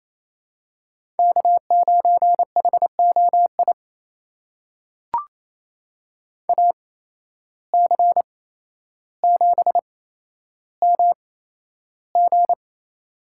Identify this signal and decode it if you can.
Morse code, heard as a keyed tone, about 27 wpm, spaced as for 12 wpm, 700 Hz. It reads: K9HOI AC7MG
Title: USA Call Signs - 5 Characters - 8x Character Spacing - 28wpm